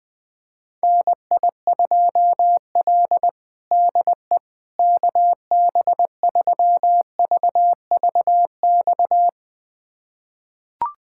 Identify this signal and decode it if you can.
Morse code, heard as a keyed tone, about 20 wpm, 700 Hz. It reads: NI2L DE KB3VVX
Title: USA Callsign DE Callsign 20wpm